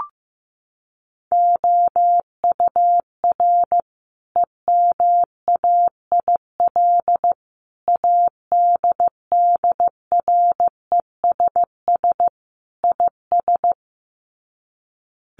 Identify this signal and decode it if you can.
Morse code, heard as a keyed tone, about 15 wpm, 700 Hz. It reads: OUR EMAIL ADDRESS IS
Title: Sets of 5 Words Encore2 15wpm